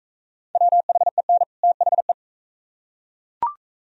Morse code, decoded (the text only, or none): WHEN THE